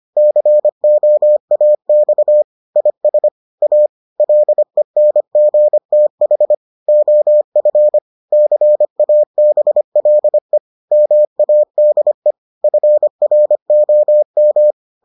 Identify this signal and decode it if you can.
Morse code, heard as a keyed tone, about 25 wpm, 600 Hz. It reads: COAX IS A LENGTH OF CABLE MADE FROM